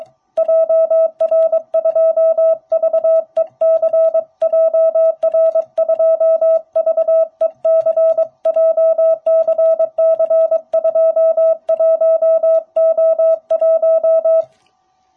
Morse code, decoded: JR2VEC JR2VEC JCC21O1

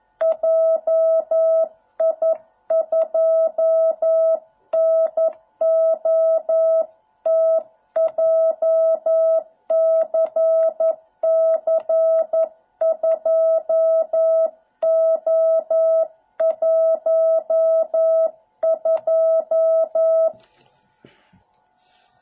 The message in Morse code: JI2NOT JCC2O12

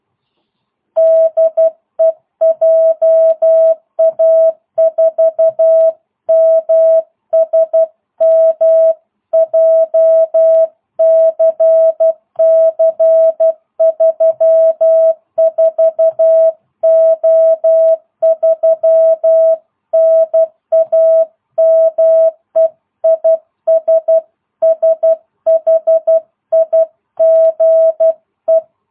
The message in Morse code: DE JA4MSM JCC 34O3 NAME IS SHIGE

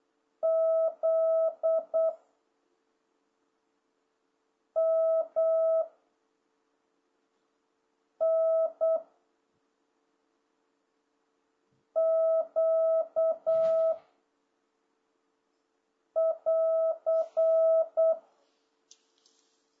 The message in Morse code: Z M N Q <AR>

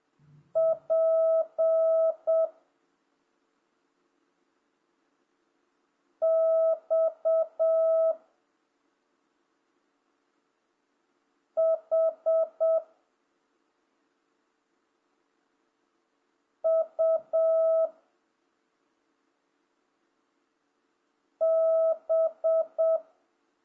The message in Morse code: P X H U B

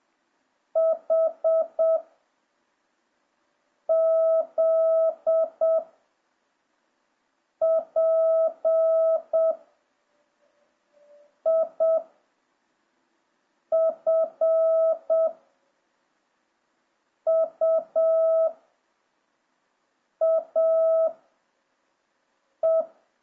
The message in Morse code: H Z P I F U A E

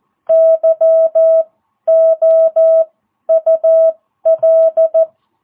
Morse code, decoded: YOUL

